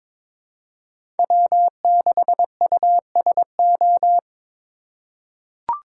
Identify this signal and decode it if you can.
Morse code, heard as a keyed tone, about 22 wpm, 700 Hz. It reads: W6USO
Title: USA Call Signs - 5 Characters - 2x Character Spacing - 22wpm